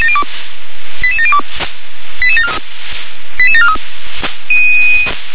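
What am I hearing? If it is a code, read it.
Five-tone selective call (ZVEI), heard as quick runs of five tones, tones of about 70 ms. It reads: 88702 88743
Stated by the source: ZVEI five-tone selective call 88633 received on SDR.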